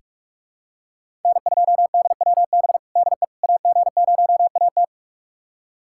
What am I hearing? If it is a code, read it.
Morse code, heard as a keyed tone, about 46 wpm, 700 Hz. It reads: N2DWB DE AG0AT
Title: USA Callsign DE Callsign - 5 Characters - 45wpm